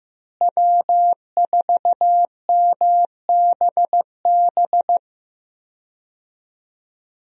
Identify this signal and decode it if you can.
Morse code, heard as a keyed tone, about 15 wpm, 700 Hz. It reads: W4MBB
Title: USA Call Signs - 5 Characters - Repeated 2x - 15wpm